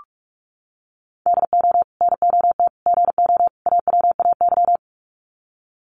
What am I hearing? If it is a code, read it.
Morse code, heard as a keyed tone, about 44 wpm, 700 Hz. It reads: DO NOT GO AWAY
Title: Sets of 4 Words 45wpm